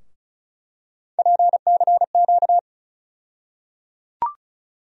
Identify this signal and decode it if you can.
Morse code, heard as a keyed tone, about 34 wpm, 700 Hz. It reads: PCQ
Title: USA Call Sign Postfixes - 3 Letters 35wpm